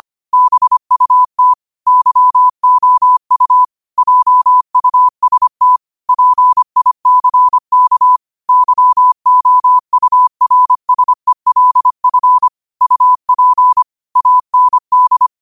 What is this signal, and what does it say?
Morse code, heard as a keyed tone, about 25 wpm, 1000 Hz. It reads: BUT YOU JUST PICK YOURSELF UP AND